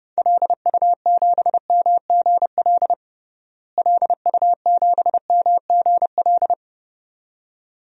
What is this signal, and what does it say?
Morse code, heard as a keyed tone, about 30 wpm, 700 Hz. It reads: LU7MGL LU7MGL
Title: DXCC-100 Call Signs - 6 Characters - Repeated 2x - 30wpm